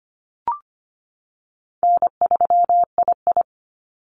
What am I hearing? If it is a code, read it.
Morse code, heard as a keyed tone, about 25 wpm, 700 Hz. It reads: N3II
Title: NA SOTA Activator Callsigns 25wpm